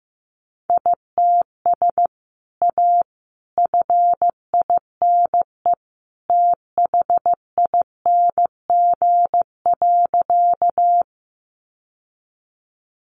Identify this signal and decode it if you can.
Morse code, heard as a keyed tone, about 15 wpm, 700 Hz. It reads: ITS A FINE THING.
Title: Sentences from Top 600 Words 15wpm